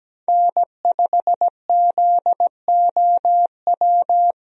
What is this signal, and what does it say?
Morse code, heard as a keyed tone, about 17 wpm, 700 Hz. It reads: N5ZOW
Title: USA Call Signs - 5 Characters - Repeated 2x - 17wpm